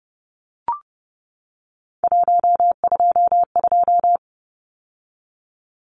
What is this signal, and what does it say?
Morse code, heard as a keyed tone, about 30 wpm, 700 Hz. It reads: 122
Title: Numbers - 3-Digit 30wpm